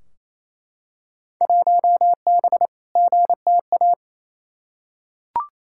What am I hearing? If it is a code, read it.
Morse code, heard as a keyed tone, about 28 wpm, 700 Hz. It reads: 1B GTA
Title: Field Day Exchanges Encore - 28wpm